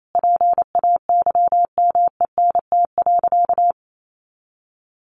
Morse code, heard as a keyed tone, about 28 wpm, 700 Hz. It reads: PAYMENT.
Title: Sentences from Top 1400 Words 28wpm